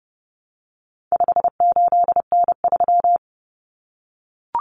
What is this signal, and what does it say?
Morse code, heard as a keyed tone, about 30 wpm, 700 Hz. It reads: 58N3
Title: Common Cut Numbers - 4-Digit 30wpm